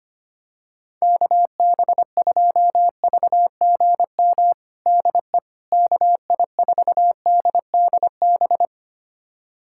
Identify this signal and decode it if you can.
Morse code, heard as a keyed tone, about 25 wpm, 700 Hz. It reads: KB2VGM DE KI4DDB
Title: USA Callsign DE Callsign - 6 Characters - 25wpm